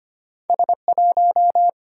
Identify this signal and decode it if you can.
Morse code, heard as a keyed tone, about 25 wpm, 700 Hz. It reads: S1
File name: Two Letter-Number Combinations - 25wpm